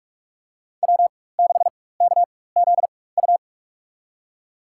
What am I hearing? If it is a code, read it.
Morse code, heard as a keyed tone, about 47 wpm, spaced as for 21 wpm, 700 Hz. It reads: W6KZU
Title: USA Call Signs - 4x Character Spacing - 45wpm